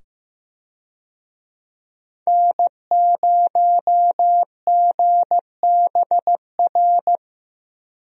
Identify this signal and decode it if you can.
Morse code, heard as a keyed tone, about 16 wpm, 700 Hz. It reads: N0GBR
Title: USA Call Signs - 5 Characters - 15wpm